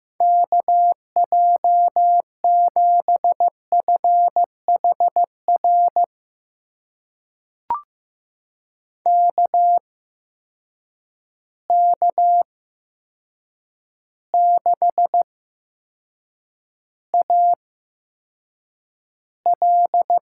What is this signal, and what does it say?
Morse code, heard as a keyed tone, about 16 wpm, spaced as for 7 wpm, 700 Hz. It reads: KJ7FHR KK6AL
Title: USA Call Signs - 6 Characters - 8x Character Spacing - 15wpm